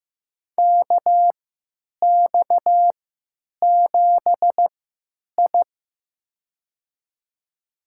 Morse code, heard as a keyed tone, about 15 wpm, spaced as for 8 wpm, 700 Hz. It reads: KX7I
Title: USA Call Signs - 4 Characters - 3x Character Spacing - 15wpm